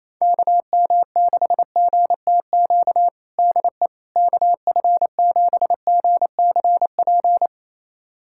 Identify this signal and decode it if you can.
Morse code, heard as a keyed tone, about 28 wpm, 700 Hz. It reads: KM6GTQ DE KF7GCP
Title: USA Callsign DE Callsign - 6 Characters - 28wpm